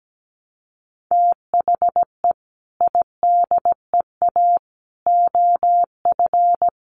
Morse code, heard as a keyed tone, about 17 wpm, 700 Hz. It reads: THE IDEA OF
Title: Sets of 3 Words 17wpm